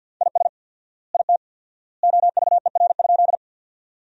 Morse code, heard as a keyed tone, about 50 wpm, 700 Hz. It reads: IS IT OVER?
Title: 4x Word-Spacing - Sentences from Top 100 Words - 50wpm